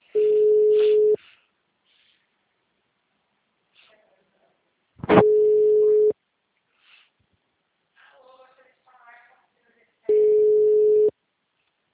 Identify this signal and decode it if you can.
Ringback tone (Europe, 425 Hz), with speech.
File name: out- -2027- - - .49397